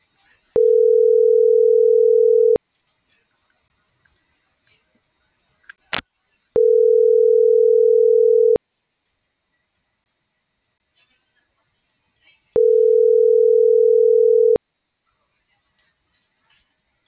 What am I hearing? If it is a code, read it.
Ringback tone (North America), 440 + 480 Hz.